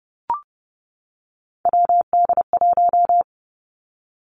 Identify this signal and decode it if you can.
Morse code, heard as a keyed tone, about 30 wpm, 700 Hz. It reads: WD1
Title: Three Challenging Letter-Number Combinations 30wpm